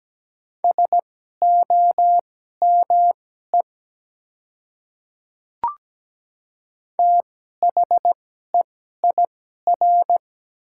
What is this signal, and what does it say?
Morse code, heard as a keyed tone, about 17 wpm, spaced as for 12 wpm, 700 Hz. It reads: SOME THEIR
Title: 2x Character Spacing - Top 200 Words - 17wpm